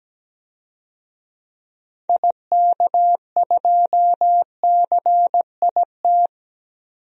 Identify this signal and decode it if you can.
Morse code, heard as a keyed tone, about 17 wpm, 700 Hz. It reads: IK2CIT
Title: DXCC-100 Call Signs - 6 Characters - 17wpm